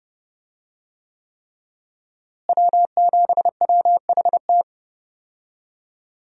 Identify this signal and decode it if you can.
Morse code, heard as a keyed tone, about 30 wpm, 700 Hz. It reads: W7WHT
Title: USA Call Signs - 5 Characters - Repeated 3x - 30wpm